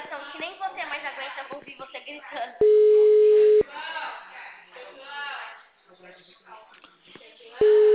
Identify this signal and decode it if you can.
Ringback tone (Europe, 425 Hz), with speech.